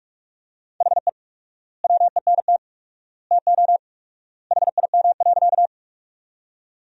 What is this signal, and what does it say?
Morse code, heard as a keyed tone, about 46 wpm, 700 Hz. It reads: HE WENT TO HIM.
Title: 4x Word-Spacing - Sentences from Top 200 Words - 45wpm